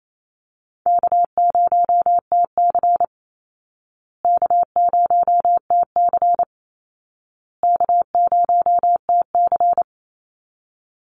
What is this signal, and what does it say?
Morse code, heard as a keyed tone, about 28 wpm, 700 Hz. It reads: K0TC K0TC K0TC
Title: Easy Call Signs Repeated 3x - 28wpm